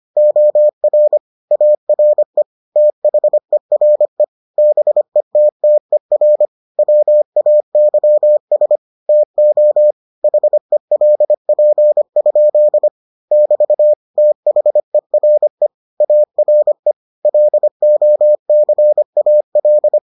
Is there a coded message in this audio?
Yes: Morse code from 0.2 to 20.0 s, heard as a keyed tone, about 25 wpm, 600 Hz. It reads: OR ARE THERE BETTER WAYS TO HELP? <BT> THERE ARE LOCAL